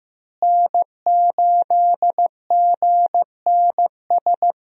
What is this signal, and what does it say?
Morse code, heard as a keyed tone, about 15 wpm, 700 Hz. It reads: N8GNS
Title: USA Call Signs - 5 Characters - Repeated 2x - 15wpm